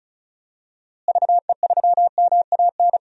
Morse code, heard as a keyed tone, about 36 wpm, 700 Hz. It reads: VE3MAN
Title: DXCC-100 Call Signs - 6 Characters - Repeated 2x - 35wpm